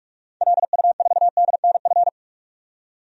Morse code, heard as a keyed tone, about 46 wpm, 700 Hz. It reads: LU4DNF
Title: DXCC-100 Call Signs - 6 Characters - 45wpm